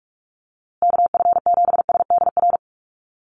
Morse code, heard as a keyed tone, about 44 wpm, 700 Hz. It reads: KF7SDR
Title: USA Call Signs Repeated 2x 45wpm